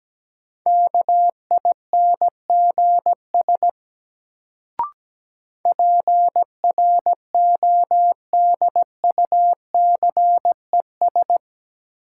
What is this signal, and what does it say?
Morse code, heard as a keyed tone, about 17 wpm, 700 Hz. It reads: KINGS PRODUCES
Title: Sentences from Top 1000 Words - Review Part 6 - New Word Drill - 17wpm